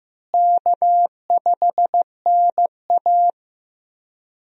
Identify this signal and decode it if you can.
Morse code, heard as a keyed tone, about 15 wpm, 700 Hz. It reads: K5NA
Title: USA Call Signs - 4 Characters - Repeated 3x - 15wpm